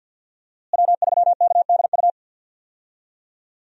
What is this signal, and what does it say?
Morse code, heard as a keyed tone, about 51 wpm, 700 Hz. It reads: W3KDU
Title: USA Call Signs - 5 Characters - Repeated 3x - 50wpm